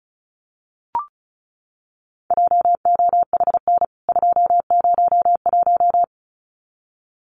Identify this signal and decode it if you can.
Morse code, heard as a keyed tone, about 34 wpm, 700 Hz. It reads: JOHN 201
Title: CWT Exchange 35wpm